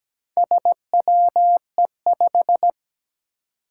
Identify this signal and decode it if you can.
Morse code, heard as a keyed tone, about 17 wpm, 700 Hz. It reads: SWE5
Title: Four Challenging Letter-Number Combinations 17wpm